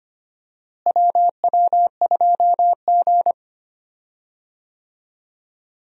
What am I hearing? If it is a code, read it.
Morse code, heard as a keyed tone, about 25 wpm, 700 Hz. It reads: WW2G
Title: USA Call Signs - 4 Characters - Repeated 2x - 25wpm